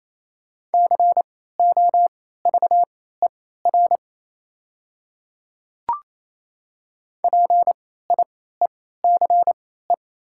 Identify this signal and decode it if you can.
Morse code, heard as a keyed tone, about 28 wpm, spaced as for 16 wpm, 700 Hz. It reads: COVER PIECE
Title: 3x Character Spacing - Top 500 Words - 28wpm